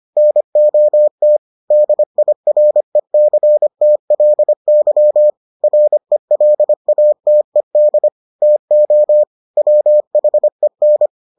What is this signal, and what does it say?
Morse code, heard as a keyed tone, about 25 wpm, 600 Hz. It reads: NOT DIRECTLY RELATED TO WHEN